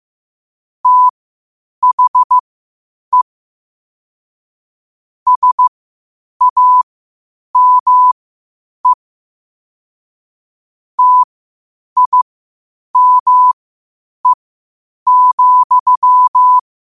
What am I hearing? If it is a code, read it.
Morse code, heard as a keyed tone, about 15 wpm, spaced as for 8 wpm, 1000 Hz. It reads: THE SAME TIME,
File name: qotdfiles / 8wpm